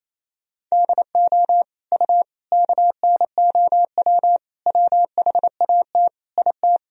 Morse code, heard as a keyed tone, about 28 wpm, 700 Hz. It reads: DO U KNOW WHAT IT
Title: Sentences from Top 100 Words and Abbreviations 28wpm